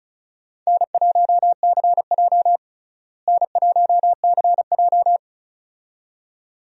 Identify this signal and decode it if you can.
Morse code, heard as a keyed tone, about 34 wpm, 700 Hz. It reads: N1CJ N1CJ
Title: USA Call Signs - 4 Characters - Repeated 2x - 35wpm